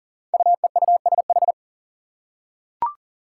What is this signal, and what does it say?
Morse code, heard as a keyed tone, about 40 wpm, 700 Hz. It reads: UEUSH
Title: Five Challenging Letter-Number Combinations 40wpm